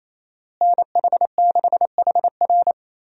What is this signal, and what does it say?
Morse code, heard as a keyed tone, about 28 wpm, 700 Hz. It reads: NH6HR
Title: USA Call Signs - 5 Characters - Repeated 3x - 28wpm